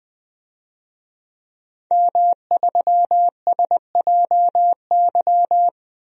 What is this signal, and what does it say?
Morse code, heard as a keyed tone, about 21 wpm, 700 Hz. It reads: M3SJY
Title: DXCC-100 Call Signs - 5 Characters - Repeated 3x - 20wpm